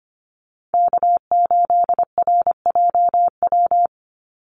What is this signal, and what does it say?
Morse code, heard as a keyed tone, about 25 wpm, 700 Hz. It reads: K8RJW
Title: Call Signs Repeated 3x - 25wpm